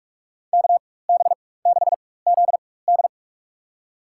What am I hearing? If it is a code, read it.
Morse code, heard as a keyed tone, about 43 wpm, spaced as for 21 wpm, 700 Hz. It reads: KB6ZD
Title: USA Call Signs - 4x Character Spacing - 45wpm